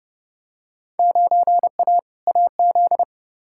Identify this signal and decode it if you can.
Morse code, heard as a keyed tone, about 32 wpm, 700 Hz. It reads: 9A AZ
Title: Field Day Exchanges - 30wpm